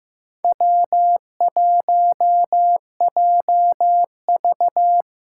Morse code, heard as a keyed tone, about 15 wpm, 700 Hz. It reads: W1JV
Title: USA Call Signs - 4 Characters - 6x Character Spacing - 15wpm